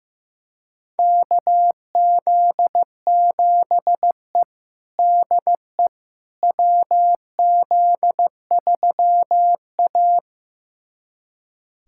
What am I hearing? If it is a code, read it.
Morse code, heard as a keyed tone, about 15 wpm, 700 Hz. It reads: KZ7E DE WZ3A
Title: USA Callsign DE Callsign - 4 Characters - 15wpm